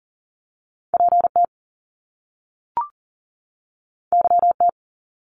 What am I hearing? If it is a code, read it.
Morse code, heard as a keyed tone, about 40 wpm, 700 Hz. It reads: PT YT
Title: Top Two Letter Combinations in Words 40wpm